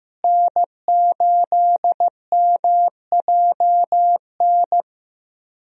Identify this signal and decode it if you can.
Morse code, heard as a keyed tone, about 15 wpm, 700 Hz. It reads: N8MJN
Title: USA Call Signs - 5 Characters - Repeated 3x - 15wpm